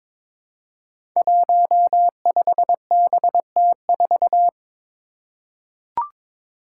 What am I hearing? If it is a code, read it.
Morse code, heard as a keyed tone, about 22 wpm, 700 Hz. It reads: 15BT4
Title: Five Challenging Letter-Number Combinations 22wpm